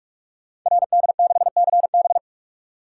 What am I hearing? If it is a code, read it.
Morse code, heard as a keyed tone, about 44 wpm, 700 Hz. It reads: RD6CB
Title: Code Groups - 5 Letter:Numbers - 45wpm